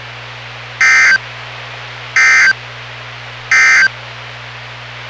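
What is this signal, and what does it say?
Emergency Alert System (SAME) tones, heard as harsh buzzing bursts. It is heard from 0.7 to 4.0 s.
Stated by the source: NOAA Weather Radio station WWG57 on 162.450 MHz, local weather broadcast.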